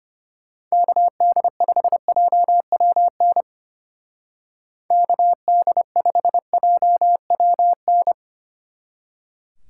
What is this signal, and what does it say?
Morse code, heard as a keyed tone, about 31 wpm, 700 Hz. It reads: KD5JWN KD5JWN